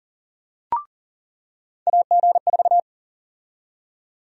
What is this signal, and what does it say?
Morse code, heard as a keyed tone, about 40 wpm, 700 Hz. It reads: AG4
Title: USA Call Sign Prefixes - 40wpm